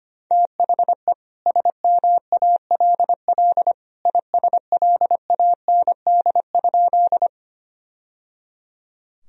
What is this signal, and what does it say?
Morse code, heard as a keyed tone, about 25 wpm, 700 Hz. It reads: THE SMALL ISLAND?
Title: Sentences from Top 1000 Words - Review Part 3 - 25wpm